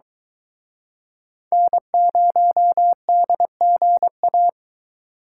Morse code, heard as a keyed tone, about 23 wpm, 700 Hz. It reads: N0DGA